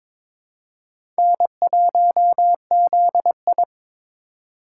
Morse code, heard as a keyed tone, about 22 wpm, 700 Hz. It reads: N1ZI
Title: USA Call Signs - 4x Character Spacing - 22wpm